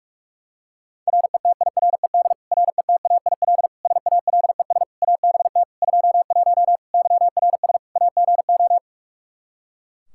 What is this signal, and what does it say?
Morse code, heard as a keyed tone, about 46 wpm, 700 Hz. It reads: RETIRED RETAIL SALES ABT 21 YRS AGO
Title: Practice QSO - Exchange 3 - More Info - 45wpm